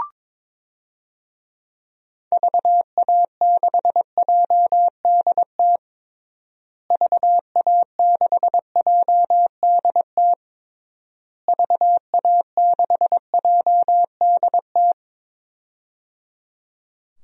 Morse code, heard as a keyed tone, about 22 wpm, 700 Hz. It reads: VA6JDT VA6JDT VA6JDT
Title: DXCC-100 Call Signs - 6 Characters - Repeated 3x - 22wpm